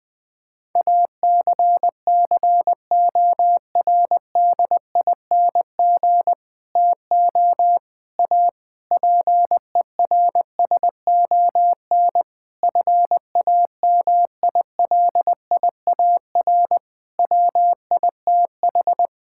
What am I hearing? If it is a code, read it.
Morse code, heard as a keyed tone, about 19 wpm, 700 Hz. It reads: ACCORDING TO A PERSON FAMILIAR WITH
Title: Sentences with 7-Word Phrases - Spoken Beginning - 20wpm